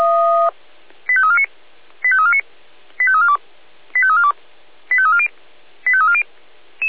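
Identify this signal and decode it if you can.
Five-tone selective call (ZVEI), heard as quick runs of five tones, tones of about 70 ms. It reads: 85349 85349 85342 85342 85340 85340
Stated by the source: Received by SDR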